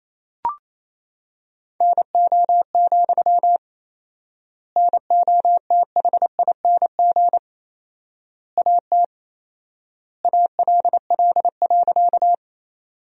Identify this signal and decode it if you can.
Morse code, heard as a keyed tone, about 28 wpm, 700 Hz. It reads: NO, NOTHING AT ALL.
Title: 4x Word-Spacing - Sentences from Top 500 Words - 28wpm